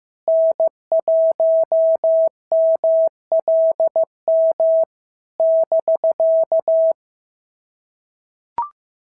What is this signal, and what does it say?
Morse code, heard as a keyed tone, about 14 wpm, 650 Hz. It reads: N1MLM <BK>
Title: SOTA QSO Practice 15wpm